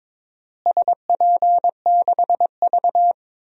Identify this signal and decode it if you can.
Morse code, heard as a keyed tone, about 21 wpm, 700 Hz. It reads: SP6V
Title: DXCC-100 Call Signs - 4 Characters - Repeated 3x - 22wpm